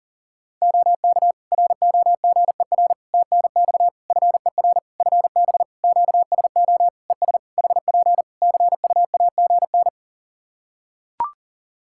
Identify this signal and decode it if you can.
Morse code, heard as a keyed tone, about 40 wpm, 700 Hz. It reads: OK ROGER TNX FER FB QSO ES HP CUAGN
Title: Practice QSO - Exchange 4 - Ending - 40wpm